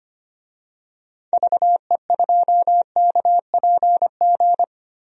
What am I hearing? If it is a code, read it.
Morse code, heard as a keyed tone, about 25 wpm, 700 Hz. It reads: VE2KPG
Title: DXCC-100 Call Signs - 6 Characters - 25wpm